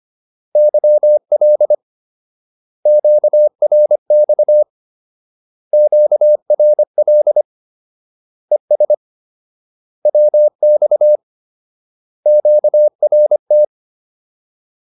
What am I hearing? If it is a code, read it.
Morse code, heard as a keyed tone, about 25 wpm, 600 Hz. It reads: YL QRX QRL ES WX QRT